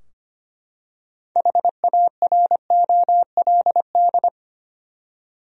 Morse code, heard as a keyed tone, about 25 wpm, 700 Hz. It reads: HAROLD
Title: USA Common Names - 25wpm